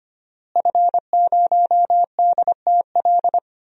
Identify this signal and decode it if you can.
Morse code, heard as a keyed tone, about 25 wpm, 700 Hz. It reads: F0DTL
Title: DXCC-100 Call Signs - 5 Characters - Repeated 3x - 25wpm